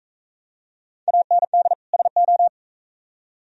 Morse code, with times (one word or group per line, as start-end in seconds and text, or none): 1.1-1.7 AND
1.9-2.5 SO